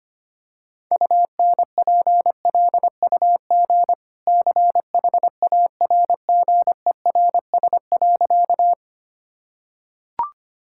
Morse code, heard as a keyed tone, about 25 wpm, 690 Hz. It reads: UNPLUG CHARGERS.
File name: Easy Sentences with Hints - 25wpm